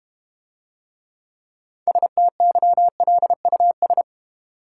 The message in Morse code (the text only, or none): STYLUS